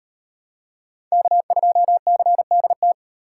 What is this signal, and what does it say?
Morse code, heard as a keyed tone, about 38 wpm, 700 Hz. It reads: K2CDT